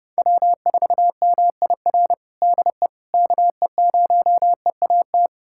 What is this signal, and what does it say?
Morse code, heard as a keyed tone, about 30 wpm, 700 Hz. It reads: W4MIR DE KE0EAT
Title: USA Callsign DE Callsign 30wpm